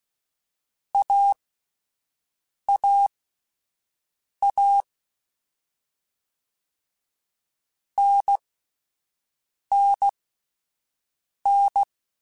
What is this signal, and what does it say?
Morse code, heard as a keyed tone, about 16 wpm, spaced as for 5 wpm, 800 Hz. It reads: AAA NNN
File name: Short Runs / Short Runs-mp3s / Unballanced @10